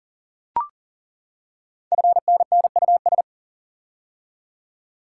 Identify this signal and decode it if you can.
Morse code, heard as a keyed tone, about 39 wpm, 700 Hz. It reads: FNNUS
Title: Code Groups - 5 Letters - 40wpm